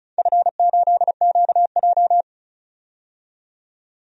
Morse code, heard as a keyed tone, about 36 wpm, 690 Hz. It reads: F8QJ
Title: DXCC-100 Call Signs - 4 Characters - 35wpm